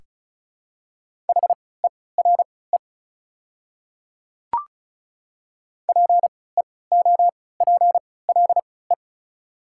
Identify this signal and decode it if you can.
Morse code, heard as a keyed tone, about 36 wpm, spaced as for 20 wpm, 690 Hz. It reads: HERE PEOPLE
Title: 3x Character Spacing - Top 200 Words - 35wpm